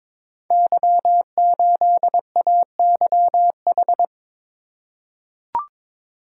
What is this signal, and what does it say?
Morse code, heard as a keyed tone, about 22 wpm, 690 Hz. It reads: Y8AYH
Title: Code Groups - 5 Letter:Numbers - 22wpm